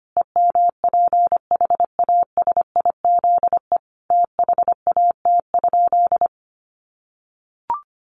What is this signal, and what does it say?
Morse code, heard as a keyed tone, about 25 wpm, 700 Hz. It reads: EMPHASIZE THAT?
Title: Sentences from Top 1500 Words 25wpm